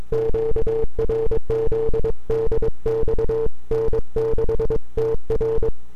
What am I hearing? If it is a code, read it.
Morse code, heard as a keyed tone, about 22 wpm, 440 Hz. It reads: QRZ DX N6TR
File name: n6tr / 12Oct2009-160